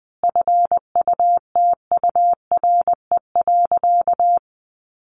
Morse code, heard as a keyed tone, about 20 wpm, 700 Hz. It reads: FUTURE.
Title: Sentences from Top 1400 Words 20wpm